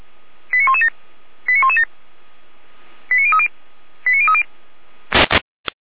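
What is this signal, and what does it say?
Five-tone selective call (ZVEI), heard as quick runs of five tones, tones of about 70 ms. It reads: 89117 89117 89030 89030
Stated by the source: ZVEI five-tone selective call received by SDR.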